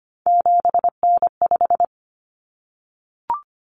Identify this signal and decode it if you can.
Morse code, heard as a keyed tone, about 25 wpm, 700 Hz. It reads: 7N5
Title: Common Cut Numbers - 3-Digit 25wpm